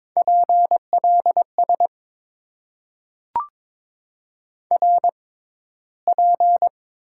Morse code, heard as a keyed tone, about 22 wpm, spaced as for 14 wpm, 700 Hz. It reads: PLS RP
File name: Top 90 QSO Elements - 6x Character Spacing - 22wpm